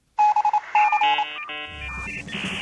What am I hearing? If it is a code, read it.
Morse code, heard as a keyed tone, about 27 wpm, 840 Hz. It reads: BC